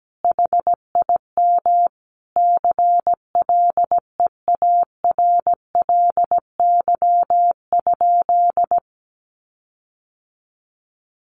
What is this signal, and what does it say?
Morse code, heard as a keyed tone, about 17 wpm, 700 Hz. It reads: HIM CLEARLY?
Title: Sentences from Top 1300 Words 17wpm